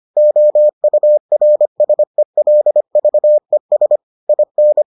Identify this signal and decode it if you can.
Morse code, heard as a keyed tone, about 25 wpm, 600 Hz. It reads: OURSELVES IN